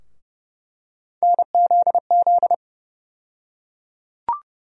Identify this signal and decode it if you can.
Morse code, heard as a keyed tone, about 29 wpm, 700 Hz. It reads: NZZ
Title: USA Call Sign Postfixes - 3 Letters 30wpm